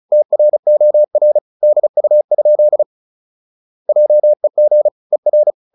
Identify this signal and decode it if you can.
Morse code, heard as a keyed tone, about 34 wpm, 600 Hz. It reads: TROR DU? JEG ER
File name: Ferie QRQ 34wpm